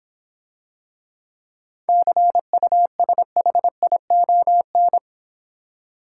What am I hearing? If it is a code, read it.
Morse code, heard as a keyed tone, about 26 wpm, 700 Hz. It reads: CUSHION